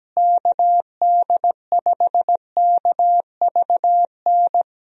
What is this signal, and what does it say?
Morse code, heard as a keyed tone, about 17 wpm, 700 Hz. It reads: KD5KVN
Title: USA Call Signs - 6 Characters - Repeated 3x - 17wpm